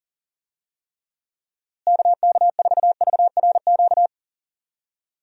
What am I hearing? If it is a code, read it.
Morse code, heard as a keyed tone, about 40 wpm, 700 Hz. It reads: KK4VRQ
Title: USA Call Signs - 6 Characters - Repeated 3x - 40wpm